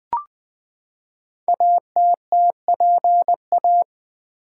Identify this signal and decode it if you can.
Morse code, heard as a keyed tone, about 20 wpm, 700 Hz. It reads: ATTPA
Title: Code Groups - 5 Letters - 20wpm